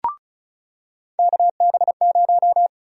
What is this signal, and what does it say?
Morse code, heard as a keyed tone, about 36 wpm, 700 Hz. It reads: KB0
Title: USA Call Sign Prefixes - 35wpm